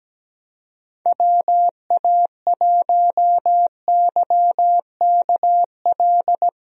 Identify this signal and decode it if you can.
Morse code, heard as a keyed tone, about 17 wpm, 700 Hz. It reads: WA1YKL
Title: USA Call Signs - 6 Characters - Repeated 2x - 17wpm